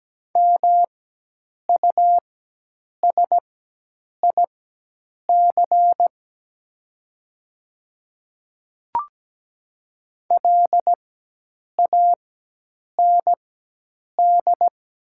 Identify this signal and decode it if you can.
Morse code, heard as a keyed tone, about 17 wpm, spaced as for 8 wpm, 700 Hz. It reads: MUSIC LAND
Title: 4x Character Spacing - Top 500 Words - 17wpm